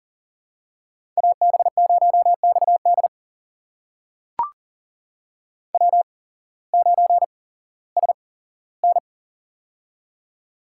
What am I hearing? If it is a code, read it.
Morse code, heard as a keyed tone, about 40 wpm, spaced as for 19 wpm, 700 Hz. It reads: AB0XD W9SN
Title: USA Call Signs - 8x Character Spacing - 40wpm